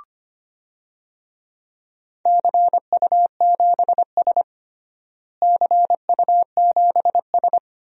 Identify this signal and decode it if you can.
Morse code, heard as a keyed tone, about 25 wpm, 700 Hz. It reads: CU7S CU7S
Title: DXCC-100 Call Signs - 4 Characters - Repeated 2x - 25wpm